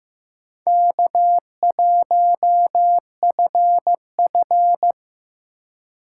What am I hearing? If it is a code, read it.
Morse code, heard as a keyed tone, about 15 wpm, 700 Hz. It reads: K1FF
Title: USA Call Signs - 4 Characters - 3x Character Spacing - 15wpm